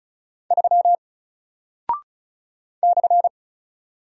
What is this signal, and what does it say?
Morse code, heard as a keyed tone, about 36 wpm, 700 Hz. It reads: 3 /
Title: Single Character (Letter:Number:Punctuation:Prosign) Alt 35wpm